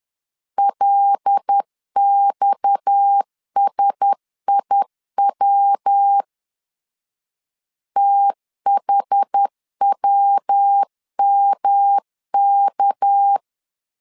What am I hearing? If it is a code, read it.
Morse code, heard as a keyed tone, about 10 wpm, 800 Hz. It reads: LXSIW THWMK
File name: CW / DET / 1 6 1500C S B 10wpm 4137